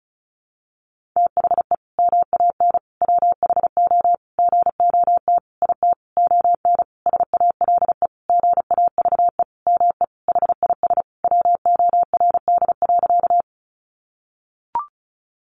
Morse code, heard as a keyed tone, about 34 wpm, 700 Hz. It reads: THE MAN WHO GOT IT ON SALE GAVE ME HIS WORD.
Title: Sentences from Top 1400 Words 35wpm